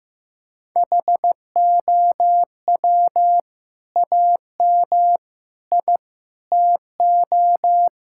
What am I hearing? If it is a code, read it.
Morse code, heard as a keyed tone, about 14 wpm, 700 Hz. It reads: HOW AM I TO
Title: Sentences from Top 1000 Words - Review Part 5 - 15wpm